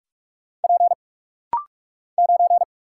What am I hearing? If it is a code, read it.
Morse code, heard as a keyed tone, about 44 wpm, 700 Hz. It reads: P 9
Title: Single Letter-Number - Rapid-Fire 45wpm